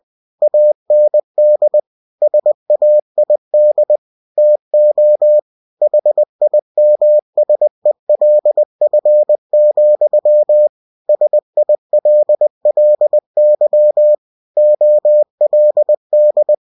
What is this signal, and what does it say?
Morse code, heard as a keyed tone, about 20 wpm, 600 Hz. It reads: AND SAID TO HIMSELF, SILLY OLD
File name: cw-books / winnie-the-pooh / winnie-20wpm-page-0020